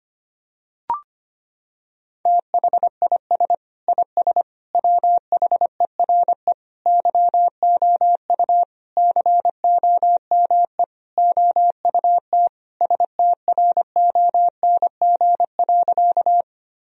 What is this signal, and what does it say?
Morse code, heard as a keyed tone, about 25 wpm, 700 Hz. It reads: THIS IS WHERE YOU COME OUT STRONG.
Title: Sentences from Top 1000 Words - Review Part 3 - 25wpm